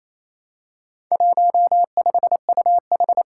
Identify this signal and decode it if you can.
Morse code, heard as a keyed tone, about 28 wpm, 700 Hz. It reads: 15UH
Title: Four Challenging Letter-Number Combinations 28wpm